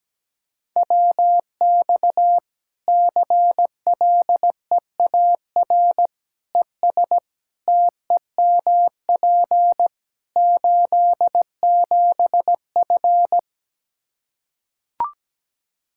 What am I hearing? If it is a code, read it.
Morse code, heard as a keyed tone, about 17 wpm, 700 Hz. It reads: WX CLEAR ES TEMP 87F
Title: Practice QSO - Exchange 2 - Info - 17wpm